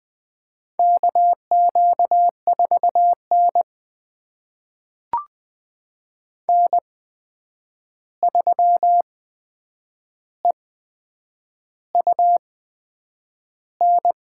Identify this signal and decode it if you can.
Morse code, heard as a keyed tone, about 20 wpm, spaced as for 8 wpm, 700 Hz. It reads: KQ4N N3EUN
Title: USA Call Signs - 8x Character Spacing - 20wpm